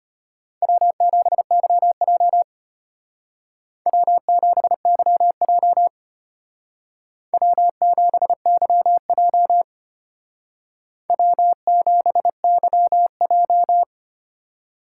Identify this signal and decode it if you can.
Morse code, heard as a keyed tone, about 32 wpm, 700 Hz. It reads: W7YJ W7YJ W7YJ W7YJ